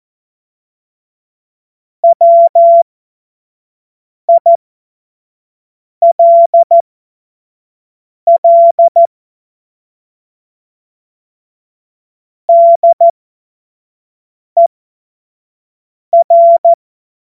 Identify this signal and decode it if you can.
Morse code, heard as a keyed tone, about 14 wpm, spaced as for 5 wpm, 680 Hz. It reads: WILL DER
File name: Morsekurs / Witze / Witz 08. August 2023 25 F70